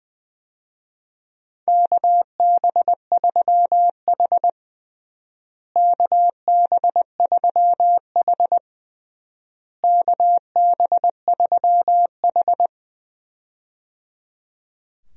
Morse code, heard as a keyed tone, about 20 wpm, 700 Hz. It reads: KB3H KB3H KB3H
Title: USA Call Signs - 4 Characters - Repeated 3x - 20wpm